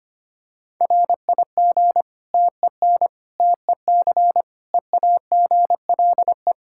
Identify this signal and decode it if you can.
Morse code, heard as a keyed tone, about 25 wpm, 700 Hz. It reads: RIG TEN TEC EAGLE
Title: Practice QSO - Exchange 2 - Info - 25wpm